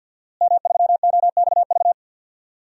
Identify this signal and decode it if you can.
Morse code, heard as a keyed tone, about 52 wpm, 700 Hz. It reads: M3OXV
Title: DXCC-100 Call Signs - 5 Characters - Repeated 3x - 50wpm